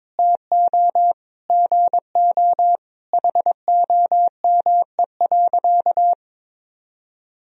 Morse code, heard as a keyed tone, about 22 wpm, 700 Hz. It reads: TO GO HOME.
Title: Sentences from Top 1000 Words - Review Part 3 - 22wpm